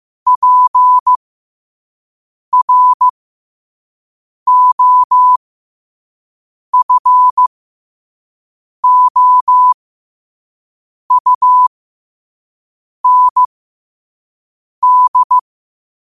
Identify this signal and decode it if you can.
Morse code, heard as a keyed tone, about 15 wpm, spaced as for 5 wpm, 1000 Hz. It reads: PROFOUND
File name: qotdfiles / 5wpm